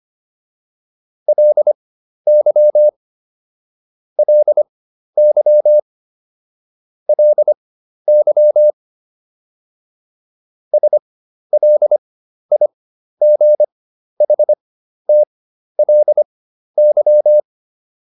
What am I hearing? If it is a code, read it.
Morse code, heard as a keyed tone, about 25 wpm, spaced as for 12 wpm, 600 Hz. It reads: LY LY LY SLIGHTLY